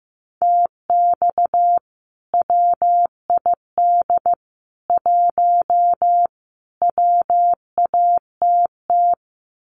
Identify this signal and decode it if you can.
Morse code, heard as a keyed tone, about 15 wpm, 700 Hz. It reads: TX WID 1 WATT
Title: Common QSO Phrases 15wpm